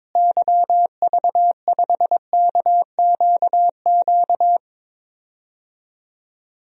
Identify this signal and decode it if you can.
Morse code, heard as a keyed tone, about 22 wpm, 700 Hz. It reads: YV5KQQ
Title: DXCC-100 Call Signs - 6 Characters - Repeated 3x - 22wpm